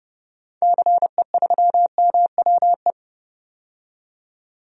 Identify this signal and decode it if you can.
Morse code, heard as a keyed tone, about 30 wpm, 700 Hz. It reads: CE3MWE
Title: DXCC-100 Call Signs - 6 Characters - 30wpm